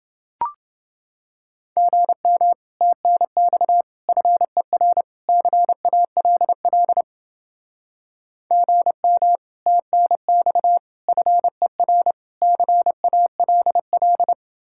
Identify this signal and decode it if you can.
Morse code, heard as a keyed tone, about 28 wpm, 700 Hz. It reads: GM TNX FER CALL GM TNX FER CALL